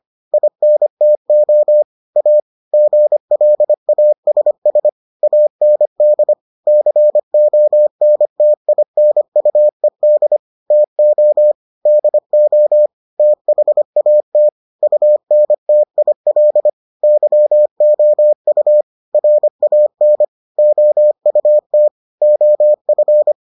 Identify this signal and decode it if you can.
Morse code, heard as a keyed tone, about 25 wpm, 600 Hz. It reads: INTO A GLASS AND CONTINUED TO DO THAT UNTIL YOU RAN OUT OF